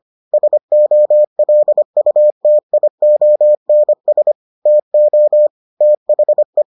Morse code, heard as a keyed tone, about 25 wpm, 600 Hz. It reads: SOLUTIONS TO THE